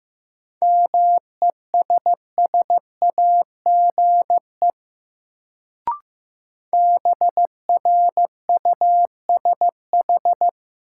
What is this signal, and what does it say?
Morse code, heard as a keyed tone, about 16 wpm, 700 Hz. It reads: MESSAGE BRUSH
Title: Sentences from Top 1100 Words - New Word Drill - 15wpm